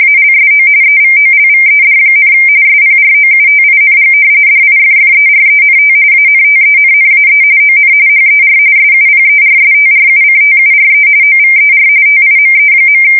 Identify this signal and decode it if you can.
RTTY, about 45.45 baud. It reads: ALSO EVOLVED, SPROUTING DIFFERENT ANTENNAE ON HER SUPERSTRUCTURE OVER TIME.
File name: fsk170 / battleshipnj